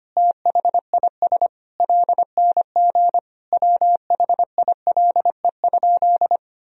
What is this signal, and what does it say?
Morse code, heard as a keyed tone, about 25 wpm, 700 Hz. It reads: THIS LNG WHILE?
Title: Sentences from Top 300 Words and Abbreviations 25wpm